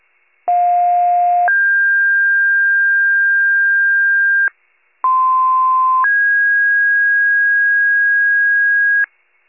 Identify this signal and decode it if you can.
Two-tone page, heard as a one-second tone, then a three-second tone, about 710 then 1600 Hz, about 1040 then 1650 Hz. It is heard from 0.5 to 9.1 s.
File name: 73 RESCUE